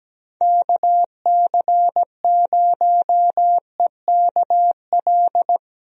Morse code, heard as a keyed tone, about 17 wpm, 700 Hz. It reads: KC0EKL
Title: USA Call Signs - 6 Characters - 17wpm